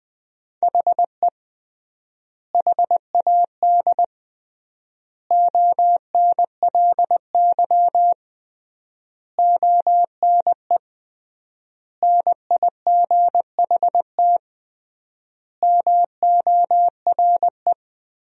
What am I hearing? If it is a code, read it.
Morse code, heard as a keyed tone, about 21 wpm, 700 Hz. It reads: HE HAD ONLY ONE NIGHT MORE
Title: 3x Word-Spacing - Sentences from Top 300 Words - 20wpm